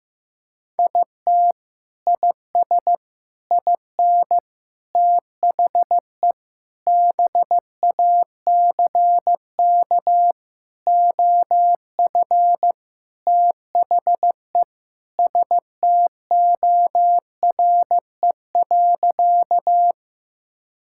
Morse code, heard as a keyed tone, about 16 wpm, 700 Hz. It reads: IT IS IN THE BACK OF THE STORE.
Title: Sentences from Top 1000 Words - Review Part 5 - 15wpm